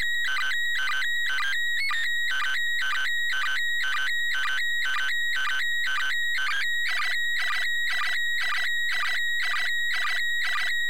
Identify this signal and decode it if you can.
Two-tone teleprinter signal, about 400 Hz shift.